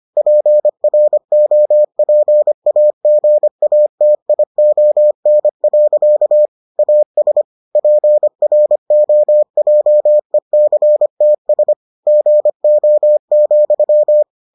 Morse code, heard as a keyed tone, about 25 wpm, 600 Hz. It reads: PROPAGATION. AS PROJECTS GO,